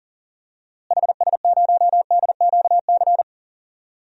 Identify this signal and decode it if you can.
Morse code, heard as a keyed tone, about 37 wpm, 700 Hz. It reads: HS0DQC
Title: DXCC-100 Call Signs - 6 Characters - Repeated 2x - 40wpm